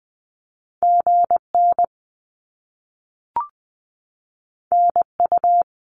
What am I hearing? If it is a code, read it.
Morse code, heard as a keyed tone, about 20 wpm, 700 Hz. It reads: GN NU
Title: Top Two Letter Combinations in Words 20wpm